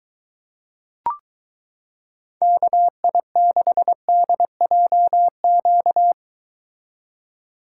Morse code, heard as a keyed tone, about 23 wpm, 700 Hz. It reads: KI6DJQ